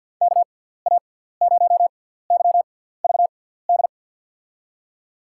Morse code, heard as a keyed tone, about 52 wpm, spaced as for 17 wpm, 700 Hz. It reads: KA0YVD
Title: USA Call Signs - 6 Characters - 6x Character Spacing - 50wpm